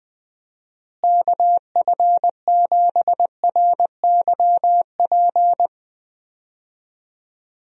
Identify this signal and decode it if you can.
Morse code, heard as a keyed tone, about 20 wpm, 700 Hz. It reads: KF7RYP
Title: USA Call Signs - 6 Characters - Repeated 3x - 20wpm